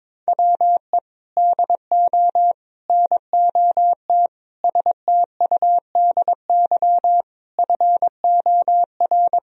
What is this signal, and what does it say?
Morse code, heard as a keyed tone, about 22 wpm, 700 Hz. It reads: WE DO NOT STUDY FOR
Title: Sentences from Top 1000 Words - Review Part 1 - 22wpm